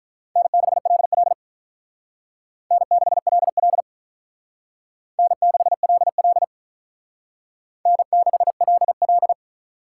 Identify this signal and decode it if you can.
Morse code, heard as a keyed tone, about 42 wpm, 700 Hz. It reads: N6LL N6LL N6LL N6LL
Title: Speed Racing - USA 1x2 Call Signs 35wpm